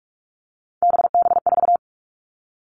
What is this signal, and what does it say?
Morse code, heard as a keyed tone, about 44 wpm, 700 Hz. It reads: BB4
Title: All Cut Numbers - 3-Digit 45wpm